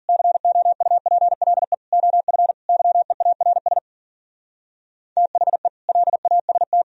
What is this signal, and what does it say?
Morse code, heard as a keyed tone, about 51 wpm, 700 Hz. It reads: COUPLE OF YEARS THE LAST